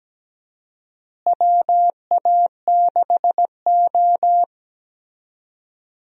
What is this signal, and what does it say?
Morse code, heard as a keyed tone, about 17 wpm, 700 Hz. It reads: WA6O
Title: USA Call Signs - 4 Characters - Repeated 2x - 17wpm